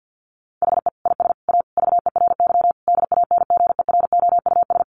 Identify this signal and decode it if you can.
Morse code, heard as a keyed tone, about 50 wpm, 700 Hz. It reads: HE IS A VERY DANGEROUS
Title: Sentences from Top 1200 Words 50wpm